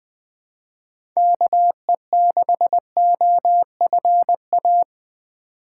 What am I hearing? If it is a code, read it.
Morse code, heard as a keyed tone, about 20 wpm, 700 Hz. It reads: KE6OFA
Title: USA Call Signs - 6 Characters - 2x Character Spacing - 20wpm